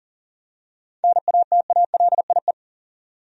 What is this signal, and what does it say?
Morse code, heard as a keyed tone, about 39 wpm, 700 Hz. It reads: NATALIE
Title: USA Common Names - 40wpm